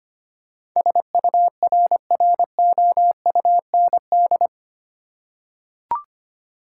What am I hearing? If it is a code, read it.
Morse code, heard as a keyed tone, about 25 wpm, 700 Hz. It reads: SURROUND
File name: 8 Letter Words - First 3 Letters Given - 25wpm